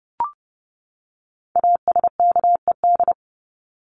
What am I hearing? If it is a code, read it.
Morse code, heard as a keyed tone, about 30 wpm, 700 Hz. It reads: ASKED
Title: ED Suffix - Top 100 Words with Base Words 30wpm